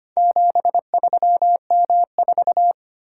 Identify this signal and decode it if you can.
Morse code, heard as a keyed tone, about 25 wpm, 700 Hz. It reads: 73M4
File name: practice / 25wpm / Four Letter-Number Combinations - 25wpm